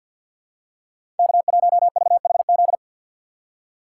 Morse code, heard as a keyed tone, about 53 wpm, 700 Hz. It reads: K1VHZ
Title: USA Call Signs - 5 Characters - 3x Character Spacing - 50wpm